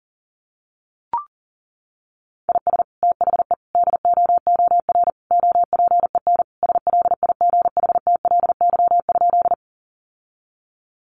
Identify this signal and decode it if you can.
Morse code, heard as a keyed tone, about 40 wpm, 700 Hz. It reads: IS THE DOOR OPEN SLIGHTLY?
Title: Sentences from Top 1300 Words 40wpm